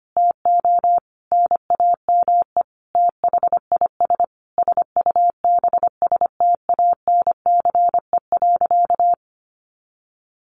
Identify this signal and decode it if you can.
Morse code, heard as a keyed tone, about 25 wpm, 700 Hz. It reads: TO NAME THIS SUBSTANCE.
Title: Sentences from Top 1500 Words 25wpm